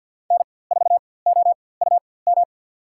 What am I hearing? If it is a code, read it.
Morse code, heard as a keyed tone, about 47 wpm, spaced as for 23 wpm, 700 Hz. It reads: N4OUM
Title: USA Call Signs - 5 Characters - 4x Character Spacing - 50wpm